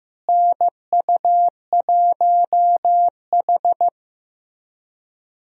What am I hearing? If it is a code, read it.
Morse code, heard as a keyed tone, about 15 wpm, 700 Hz. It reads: NU1H
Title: USA Call Signs - 4 Characters - Repeated 3x - 15wpm